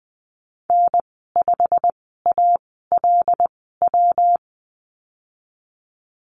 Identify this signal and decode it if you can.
Morse code, heard as a keyed tone, about 20 wpm, spaced as for 15 wpm, 700 Hz. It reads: N5ALW
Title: 140 Lesson 23g - Letter f - Call Signs 20wpm